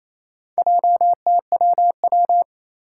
Morse code, heard as a keyed tone, about 27 wpm, 700 Hz. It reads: JTWW
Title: Four Challenging Letter-Number Combinations 28wpm